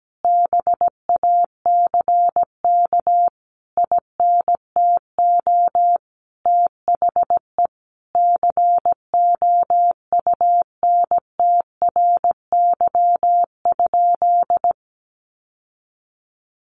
Morse code, heard as a keyed tone, about 17 wpm, 700 Hz. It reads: BACK INTO THE COUNTRY?
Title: Sentences from Top 1400 Words 17wpm